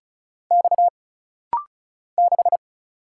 Morse code, heard as a keyed tone, about 35 wpm, 700 Hz. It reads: X 6
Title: Single Letter-Number - Rapid-Fire 35wpm